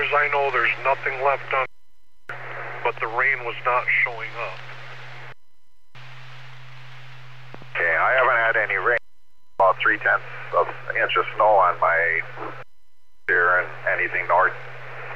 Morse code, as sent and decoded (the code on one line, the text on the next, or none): none